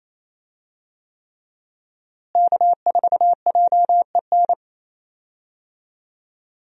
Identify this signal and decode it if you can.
Morse code, heard as a keyed tone, about 28 wpm, 700 Hz. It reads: K4JEN
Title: USA Call Signs - 5 Characters - Repeated 3x - 28wpm